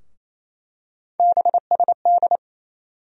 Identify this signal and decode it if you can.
Morse code, heard as a keyed tone, about 28 wpm, 700 Hz. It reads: BSD
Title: USA Call Sign Postfixes - 3 Letters 28wpm